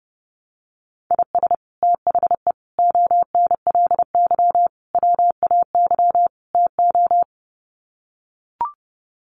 Morse code, heard as a keyed tone, about 30 wpm, 700 Hz. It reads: IS THE ONLY WAY TO
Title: Sets of 5 Words Encore 30wpm